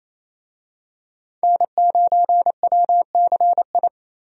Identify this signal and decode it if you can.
Morse code, heard as a keyed tone, about 28 wpm, 700 Hz. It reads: N9WCI
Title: USA Call Signs - 5 Characters - 28wpm